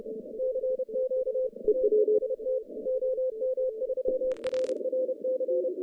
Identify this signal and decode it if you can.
Morse code, heard as a keyed tone, about 31 wpm, 520 Hz. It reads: CQ TEST OM3RRC
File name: 2015 cq ww cw / 2015 cqwww k5tr - 0-27